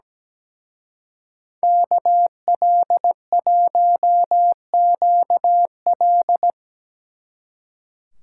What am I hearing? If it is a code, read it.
Morse code, heard as a keyed tone, about 19 wpm, 700 Hz. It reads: KL1QL
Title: Speed Racing - USA 2x2 Call Signs 17wpm